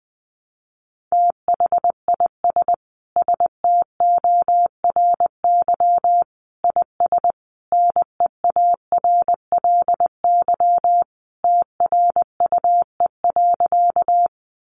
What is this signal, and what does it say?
Morse code, heard as a keyed tone, about 20 wpm, 700 Hz. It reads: THIS STORY IS NEARLY TRUE.
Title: Sentences from Top 1300 Words 20wpm